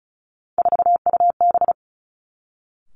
Morse code, heard as a keyed tone, about 36 wpm, 700 Hz. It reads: 4UB
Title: Three Challenging Letter-Number Combinations 35wpm